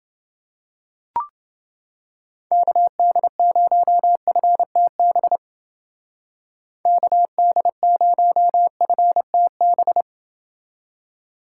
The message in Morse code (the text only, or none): KD0FTB KD0FTB